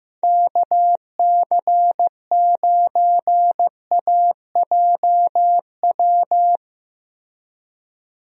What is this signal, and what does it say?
Morse code, heard as a keyed tone, about 15 wpm, 700 Hz. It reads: KC9AJW
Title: USA Call Signs - 6 Characters - Repeated 3x - 15wpm